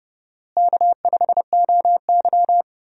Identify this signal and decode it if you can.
Morse code, heard as a keyed tone, about 30 wpm, 700 Hz. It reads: K5OY
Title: USA Call Signs - 4 Characters - Repeated 2x - 30wpm